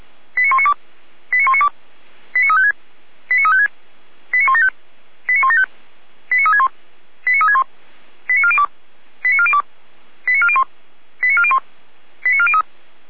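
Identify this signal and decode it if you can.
Five-tone selective call (ZVEI), heard as quick runs of five tones, tones of about 80 ms. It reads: 89192 89192 89356 89356 89175 89175 89361 89361 89402 89402 89401 89401 89403
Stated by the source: ZVEI five-tone selective call received on SDR.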